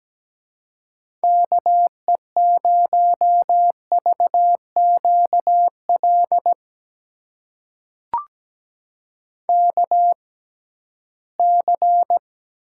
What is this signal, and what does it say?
Morse code, heard as a keyed tone, about 17 wpm, spaced as for 13 wpm, 700 Hz. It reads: KE0VQL KC
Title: USA Call Signs - 6 Characters - 6x Character Spacing - 17wpm